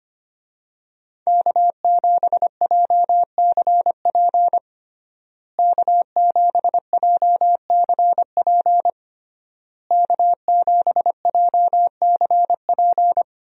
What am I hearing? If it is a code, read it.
Morse code, heard as a keyed tone, about 25 wpm, 700 Hz. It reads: K7JCP K7JCP K7JCP
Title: USA Call Signs - 5 Characters - Repeated 3x - 25wpm